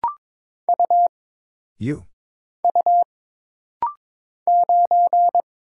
Morse code, heard as a keyed tone, about 22 wpm, 700 Hz. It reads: U U 9
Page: This rapid-fire timing is geared to push your ICR (Instant Character Recognition) skills.